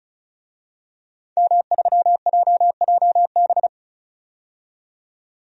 Morse code, heard as a keyed tone, about 36 wpm, 700 Hz. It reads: M3JJB
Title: DXCC-100 Call Signs - 5 Characters - 35wpm